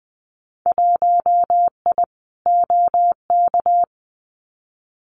Morse code, heard as a keyed tone, about 20 wpm, 700 Hz. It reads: 1I OK
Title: Winter Field Day Exchanges Encore 20wpm